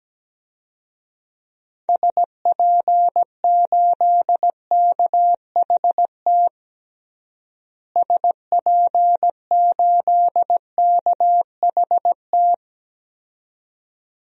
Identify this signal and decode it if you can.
Morse code, heard as a keyed tone, about 17 wpm, 700 Hz. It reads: SP8KHT SP8KHT
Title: DXCC-100 Call Signs - 6 Characters - Repeated 2x - 17wpm